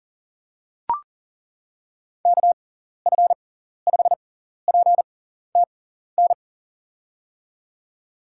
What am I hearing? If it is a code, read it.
Morse code, heard as a keyed tone, about 40 wpm, spaced as for 14 wpm, 700 Hz. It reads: KF5PTN
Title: USA Call Signs - 6 Characters - 6x Character Spacing - 40wpm